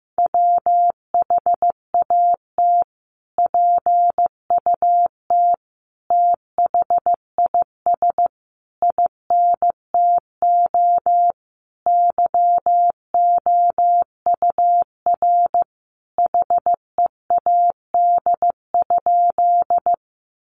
Morse code, heard as a keyed tone, about 14 wpm, 700 Hz. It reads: WHAT PUT THIS INTO YOUR HEAD?
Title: Sentences from Top 600 Words 15wpm